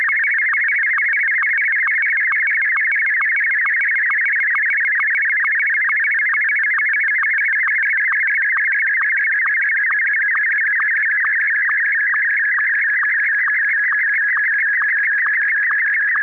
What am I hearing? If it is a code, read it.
SSTV (slow-scan television), Martin 1.